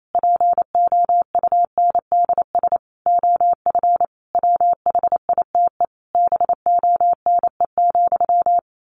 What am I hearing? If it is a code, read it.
Morse code, heard as a keyed tone, about 28 wpm, 700 Hz. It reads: POUNDS OF WHITE BONE,
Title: A Sound Of Thunder 28wpm